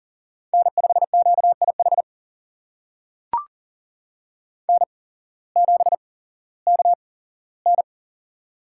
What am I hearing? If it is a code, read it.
Morse code, heard as a keyed tone, about 38 wpm, spaced as for 18 wpm, 700 Hz. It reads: N5QIH N7KN
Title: USA Call Signs - 8x Character Spacing - 40wpm